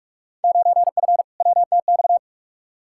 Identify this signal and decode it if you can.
Morse code, heard as a keyed tone, about 43 wpm, 700 Hz. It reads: 9F WTX
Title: Field Day Exchanges Encore - 45wpm